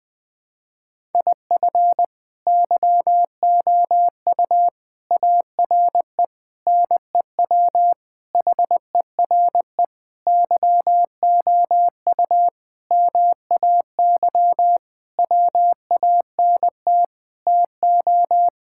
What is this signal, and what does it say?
Morse code, heard as a keyed tone, about 21 wpm, 700 Hz. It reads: IF YOU ARE NEW HERE YOU MAY WANT TO
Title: Sentences with 8-Word Phrases - 20wpm